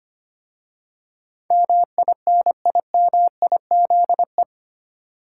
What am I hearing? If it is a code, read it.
Morse code, heard as a keyed tone, about 25 wpm, 700 Hz. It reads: MINIMIZE
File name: 8 Letter Words - First 3 Letters Given - 25wpm